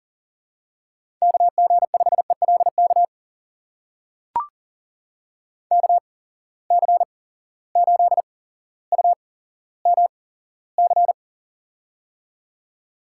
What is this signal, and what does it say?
Morse code, heard as a keyed tone, about 39 wpm, spaced as for 17 wpm, 700 Hz. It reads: KG5ELK KC8UMC
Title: USA Call Signs - 6 Characters - 8x Character Spacing - 40wpm